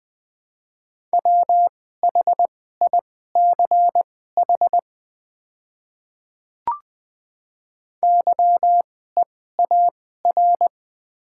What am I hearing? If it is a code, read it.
Morse code, heard as a keyed tone, about 21 wpm, spaced as for 15 wpm, 700 Hz. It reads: WHICH YEAR
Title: 2x Character Spacing - Top 200 Words - 20wpm